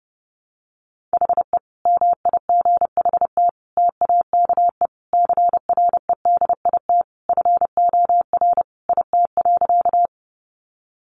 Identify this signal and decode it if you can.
Morse code, heard as a keyed tone, about 30 wpm, 700 Hz. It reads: HE MIGHT TAKE CREDIT FOR IT.
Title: Sentences from Top 1000 Words 30wpm